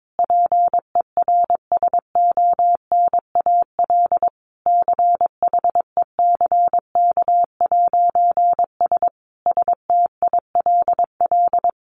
Morse code, heard as a keyed tone, about 22 wpm, 700 Hz. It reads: PERSONAL CHECK'S STILL
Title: A Sound Of Thunder 22wpm